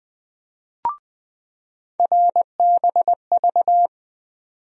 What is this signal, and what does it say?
Morse code, heard as a keyed tone, about 21 wpm, 700 Hz. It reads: RBV
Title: USA Call Sign Postfixes - 3 Letters 20wpm